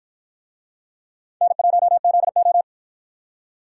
Morse code, heard as a keyed tone, about 52 wpm, 700 Hz. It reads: N1ZO